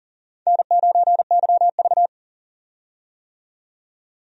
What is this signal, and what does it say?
Morse code, heard as a keyed tone, about 40 wpm, 700 Hz. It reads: N9YV
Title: USA Call Signs - 4 Characters - 40wpm